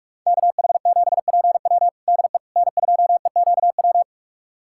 Morse code, heard as a keyed tone, about 46 wpm, 700 Hz. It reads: KH7PW DE N2EQW
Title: USA Callsign DE Callsign - 5 Characters - 45wpm